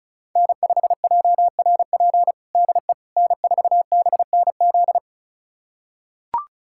Morse code, heard as a keyed tone, about 36 wpm, 700 Hz. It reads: N5JRP DE N4BNZ
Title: USA Callsign DE Callsign - 5 Characters - 35wpm